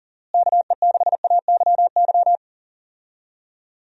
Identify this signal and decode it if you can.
Morse code, heard as a keyed tone, about 40 wpm, 700 Hz. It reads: KE6AYY
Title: USA Call Signs - 6 Characters - Repeated 2x - 40wpm